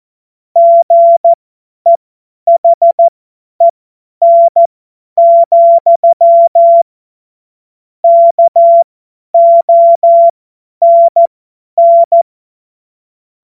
Morse code, heard as a keyed tone, about 14 wpm, spaced as for 10 wpm, 680 Hz. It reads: GEHEN, KONN